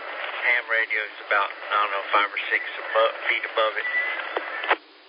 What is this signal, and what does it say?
Emergency Alert System (SAME) tones, heard as harsh buzzing bursts, with speech. It is heard from 0.8 to 4.3 s.